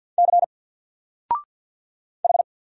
Morse code, heard as a keyed tone, about 49 wpm, 700 Hz. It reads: C H
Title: Single Character (Letter:Number:Punctuation:Prosign) 50wpm